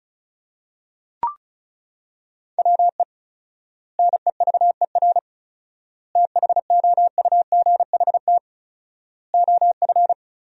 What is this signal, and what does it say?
Morse code, heard as a keyed tone, about 36 wpm, 700 Hz. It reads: WE NEVER THOUGHT OF
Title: 4x Word-Spacing - Sentences from Top 300 Words - 35wpm